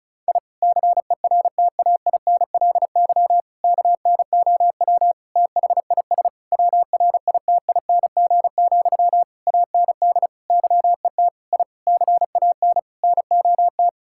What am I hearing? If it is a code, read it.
Morse code, heard as a keyed tone, about 34 wpm, 700 Hz. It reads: I CERTAINLY KNOW THIS WRITING, AND YET I CAN NOT
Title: Sentences from Top 1000 Words - Review Part 6 - 35wpm